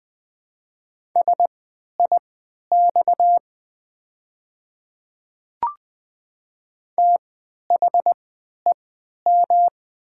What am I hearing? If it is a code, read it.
Morse code, heard as a keyed tone, about 20 wpm, spaced as for 11 wpm, 700 Hz. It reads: SIX THEM
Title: 3x Character Spacing - Top 500 Words - 20wpm